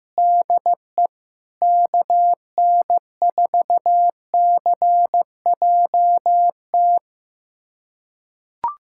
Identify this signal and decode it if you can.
Morse code, heard as a keyed tone, about 15 wpm, 700 Hz. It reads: DE KN4CJT
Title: USA Callsign DE Callsign - 6 Characters - 15wpm